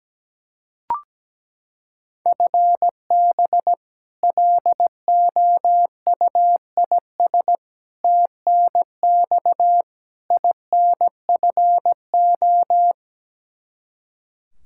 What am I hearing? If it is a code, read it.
Morse code, heard as a keyed tone, about 17 wpm, 700 Hz. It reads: FB LOUIS TNX INFO
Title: K1USN SST QSO Practice 17wpm